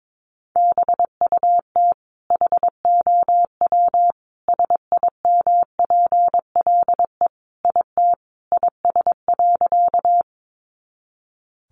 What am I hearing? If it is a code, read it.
Morse code, heard as a keyed tone, about 22 wpm, 700 Hz. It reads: BUT HOW SIMPLE IT IS.
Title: Sentences from Top 400 Words 22wpm